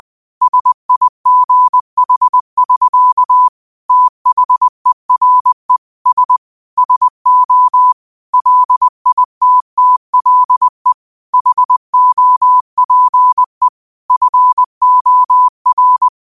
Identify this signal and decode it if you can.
Morse code, heard as a keyed tone, about 20 wpm, 1000 Hz. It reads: SIGH<SK> THERE S SO LITTLE HOPE FOR
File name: qotdfiles / 20wpm